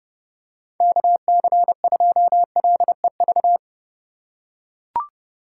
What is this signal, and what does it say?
Morse code, heard as a keyed tone, about 30 wpm, 700 Hz. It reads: KC2LEV
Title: USA Call Signs - 6 Characters - 2x Character Spacing - 30wpm